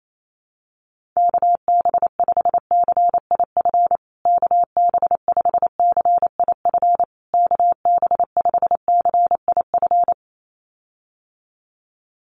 Morse code, heard as a keyed tone, about 28 wpm, 700 Hz. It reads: KB5CIF KB5CIF KB5CIF
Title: USA Call Signs Repeated 3x 28wpm